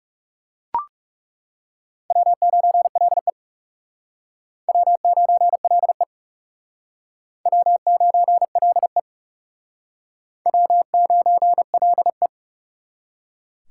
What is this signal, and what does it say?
Morse code, heard as a keyed tone, about 37 wpm, 700 Hz. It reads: W9LE W9LE W9LE W9LE
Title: Speed Racing - USA 1x2 Call Signs 30wpm